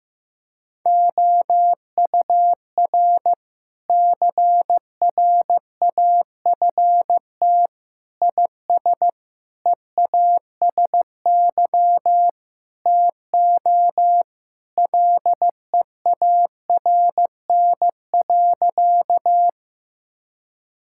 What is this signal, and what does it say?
Morse code, heard as a keyed tone, about 15 wpm, 700 Hz. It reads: OUR CRAFT IS EASY TO LEARN.
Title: Sentences from Top 1000 Words - Review Part 6 - 15wpm